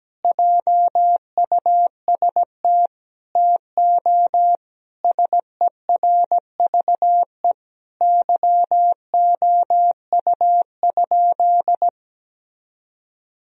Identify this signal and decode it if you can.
Morse code, heard as a keyed tone, about 17 wpm, 700 Hz. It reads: JUST TO SERVE YOU?
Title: Sentences from Top 1000 Words - Review Part 2 - 17wpm